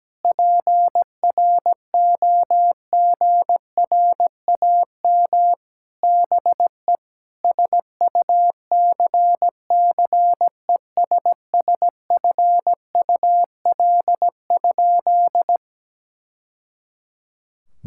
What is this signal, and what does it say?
Morse code, heard as a keyed tone, about 17 wpm, 700 Hz. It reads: PROGRAM BE SUCCESSFUL?
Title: Sentences from Top 1000 Words - Review Part 5 - 17wpm